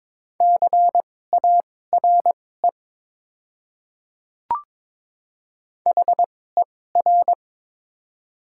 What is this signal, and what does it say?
Morse code, heard as a keyed tone, about 22 wpm, spaced as for 16 wpm, 690 Hz. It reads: CARE HER
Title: 2x Character Spacing - Top 500 Words - 22wpm